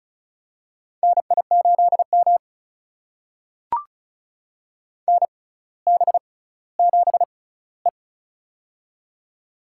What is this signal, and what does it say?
Morse code, heard as a keyed tone, about 36 wpm, spaced as for 18 wpm, 700 Hz. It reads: NI8M NB7E
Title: USA Call Signs - 4 Characters - 6x Character Spacing - 35wpm